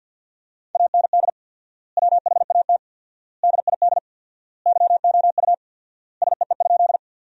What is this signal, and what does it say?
Morse code, heard as a keyed tone, about 52 wpm, 700 Hz. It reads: AND WHAT DID YOU SEE?
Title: 4x Word-Spacing - Sentences from Top 100 Words - 50wpm